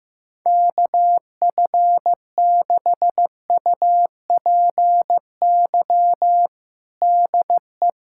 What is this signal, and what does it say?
Morse code, heard as a keyed tone, about 15 wpm, 700 Hz. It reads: KF6UPY DE
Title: USA Callsign DE Callsign - 6 Characters - 15wpm